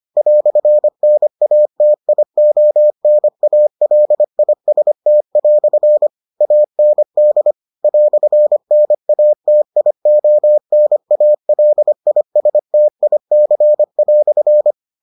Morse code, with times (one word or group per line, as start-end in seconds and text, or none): 0.2-6.1 "NATIONALIST"
6.4-7.5 AND
7.8-14.7 "NATIONALISTIC"